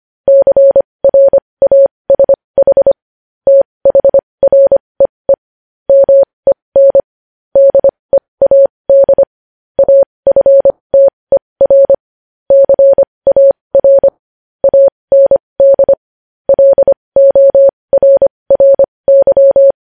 Morse code, heard as a keyed tone, about 25 wpm, spaced as for 20 wpm, 550 Hz. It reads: CRASH THREE MEN DEAD AFTER CAR AND LORRY
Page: News Headlines in Morse Code at 20 WPM